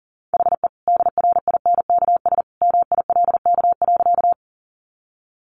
Morse code, heard as a keyed tone, about 40 wpm, 700 Hz. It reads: HE DRINKS MILK.
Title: Easy Phrases 40wpm